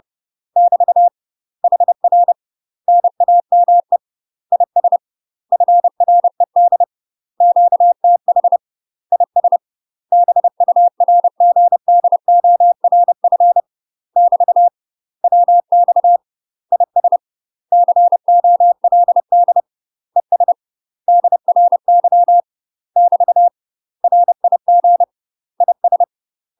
Morse code, heard as a keyed tone, about 30 wpm, 700 Hz. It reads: <BT> HR NAME IS FRED QTH IS BURGDORF <BT> WX IS COLD ES DRY <BT> RIG IS
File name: UK MASTER QSOs TRAINING / 30WPM / UK MASTER QSOs 0033